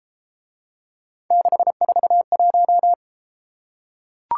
Morse code, heard as a keyed tone, about 31 wpm, 700 Hz. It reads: 641